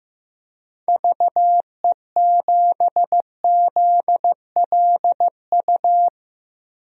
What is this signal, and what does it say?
Morse code, heard as a keyed tone, about 15 wpm, 700 Hz. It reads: VE7ZLU
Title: DXCC-100 Call Signs - 6 Characters - Repeated 3x - 15wpm